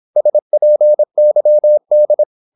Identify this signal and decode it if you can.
Morse code, heard as a keyed tone, about 26 wpm, 600 Hz. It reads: SPYD
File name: Korte ord DK 26wpm